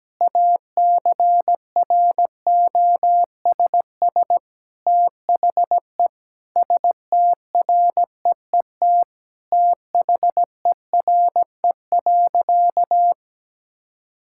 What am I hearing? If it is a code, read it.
Morse code, heard as a keyed tone, about 17 wpm, 700 Hz. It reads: ACROSS THE STREET THERE.
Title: Sentences from Top 1000 Words - Review Part 5 - 17wpm